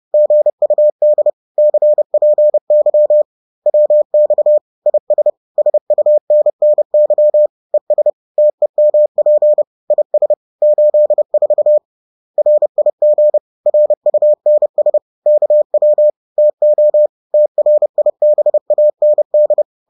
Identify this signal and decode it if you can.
Morse code, heard as a keyed tone, about 30 wpm, 600 Hz. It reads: GUD CPY WX IS SUNNY ES TEMP IS 84 RIG RUNS KW TO TRIBAND